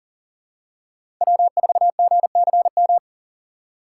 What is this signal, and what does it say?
Morse code, heard as a keyed tone, about 40 wpm, 700 Hz. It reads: W4GCM
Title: USA Call Signs - 2x Character Spacing - 40wpm